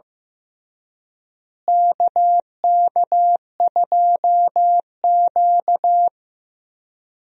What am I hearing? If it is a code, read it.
Morse code, heard as a keyed tone, about 15 wpm, 700 Hz. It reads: KK2Q
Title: USA Call Signs - 4 Characters - Repeated 3x - 15wpm